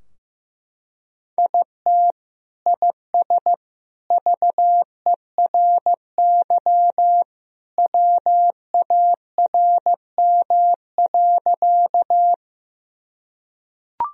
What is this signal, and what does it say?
Morse code, heard as a keyed tone, about 16 wpm, 700 Hz. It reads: IT IS VERY WARM.
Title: Sentences from Top 1000 Words - Review Part 2 - 15wpm